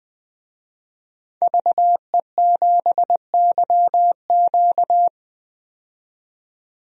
Morse code, heard as a keyed tone, about 21 wpm, 700 Hz. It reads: VE7YQ
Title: DXCC-100 Call Signs - 5 Characters - 20wpm